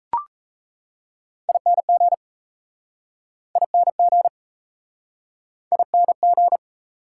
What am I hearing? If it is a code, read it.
Morse code, heard as a keyed tone, about 37 wpm, 690 Hz. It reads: ING ING ING